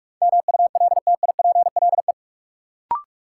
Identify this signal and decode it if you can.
Morse code, heard as a keyed tone, about 43 wpm, 700 Hz. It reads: MULTIPLE
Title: Sets of 5 Words - New Word Drill - 45wpm